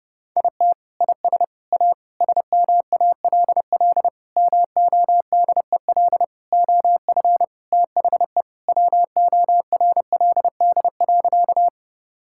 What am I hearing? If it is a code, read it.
Morse code, heard as a keyed tone, about 30 wpm, 700 Hz. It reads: IT IS A SMALL MODEL OF THE WORLD.
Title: Sentences from Top 1000 Words - Review Part 5 - 30wpm